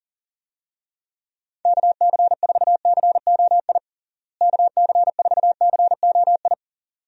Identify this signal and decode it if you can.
Morse code, heard as a keyed tone, about 40 wpm, 700 Hz. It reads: KC4COI KC4COI
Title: USA Call Signs - 6 Characters - Repeated 2x - 40wpm